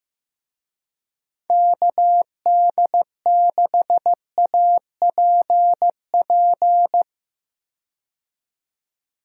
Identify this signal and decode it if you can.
Morse code, heard as a keyed tone, about 15 wpm, 700 Hz. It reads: KD6APP
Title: USA Call Signs - 6 Characters - 15wpm